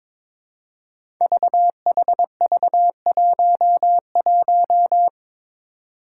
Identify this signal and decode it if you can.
Morse code, heard as a keyed tone, about 22 wpm, 700 Hz. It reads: VHV11
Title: Five Challenging Letter-Number Combinations 22wpm